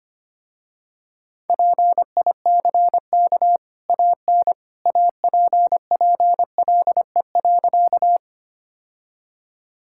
Morse code, heard as a keyed tone, about 25 wpm, 700 Hz. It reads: PICK AN APPLE.
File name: Easy Sentences with Hints - 25wpm